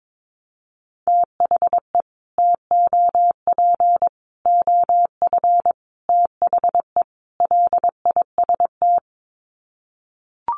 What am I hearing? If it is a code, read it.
Morse code, heard as a keyed tone, about 22 wpm, 700 Hz. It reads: THE TOP OF THE LIST
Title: Sets of 5 Words Encore2 22wpm